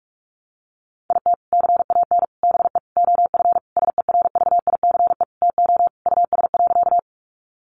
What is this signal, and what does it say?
Morse code, heard as a keyed tone, about 44 wpm, 700 Hz. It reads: IT CAN BE OF SERVICE TO US.
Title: Sentences from Top 1200 Words 45wpm